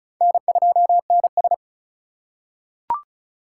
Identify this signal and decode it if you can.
Morse code, heard as a keyed tone, about 36 wpm, 700 Hz. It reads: N2NS
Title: K1USN SST QSO Practice - 35wpm